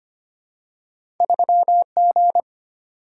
Morse code, heard as a keyed tone, about 25 wpm, 700 Hz. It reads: 3G
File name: Two Letter-Number Combinations - 25wpm